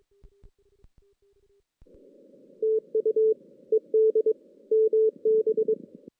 Morse code, heard as a keyed tone, about 24 wpm, 410 Hz. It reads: TU ED MB
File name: 2016 JAN CW NAQP / 2016 Jan naqp k5ot - 0-6